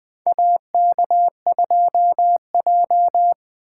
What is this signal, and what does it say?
Morse code, heard as a keyed tone, about 20 wpm, 700 Hz. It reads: AK2J
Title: USA Call Signs - 4x Character Spacing - 20wpm